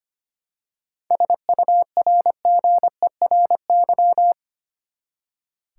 Morse code, heard as a keyed tone, about 25 wpm, 700 Hz. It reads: SURGERY
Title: Seven Letter Words - 25wpm